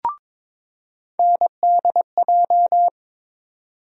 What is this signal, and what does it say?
Morse code, heard as a keyed tone, about 22 wpm, 700 Hz. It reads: NDJ
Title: USA Call Sign Postfixes - 3 Letters 22wpm